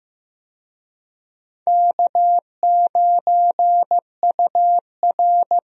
Morse code, heard as a keyed tone, about 15 wpm, 700 Hz. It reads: K9UR
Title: USA Call Signs - 4 Characters - Repeated 2x - 15wpm